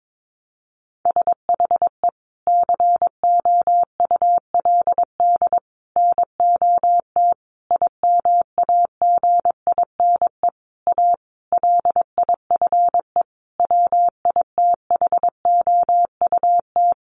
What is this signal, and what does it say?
Morse code, heard as a keyed tone, about 22 wpm, 700 Hz. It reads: SHE COULD NOT IMAGINE A LIFE WITHOUT
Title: Sentences from Top 1400 Words 22wpm